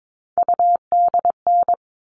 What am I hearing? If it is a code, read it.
Morse code, heard as a keyed tone, about 22 wpm, 700 Hz. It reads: UDN
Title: All Cut Numbers - 3-Digit 22wpm